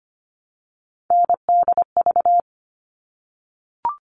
Morse code, heard as a keyed tone, about 25 wpm, 700 Hz. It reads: NDV
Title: All Cut Numbers - 3-Digit 25wpm